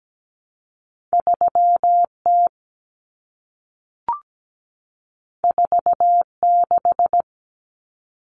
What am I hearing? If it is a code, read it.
Morse code, heard as a keyed tone, about 17 wpm, 700 Hz. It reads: 3T 46
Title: Common Cut Numbers - 2-Digit 17wpm